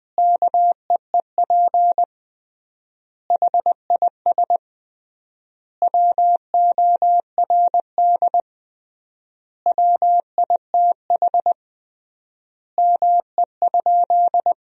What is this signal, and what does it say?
Morse code, heard as a keyed tone, about 20 wpm, 700 Hz. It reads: KEEP HIS WORD WITH ME?
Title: 3x Word-Spacing - Sentences from Top 300 Words - 20wpm